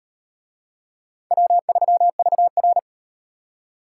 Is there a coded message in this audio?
Yes: Morse code from 1.3 to 2.8 s, heard as a keyed tone, about 38 wpm, 700 Hz. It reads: W3VR